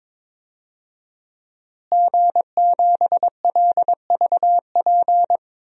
Morse code, heard as a keyed tone, about 21 wpm, 700 Hz. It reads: G7LVP
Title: DXCC-100 Call Signs - 5 Characters - Repeated 2x - 22wpm